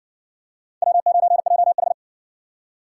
Morse code, heard as a keyed tone, about 51 wpm, 700 Hz. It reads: W9JH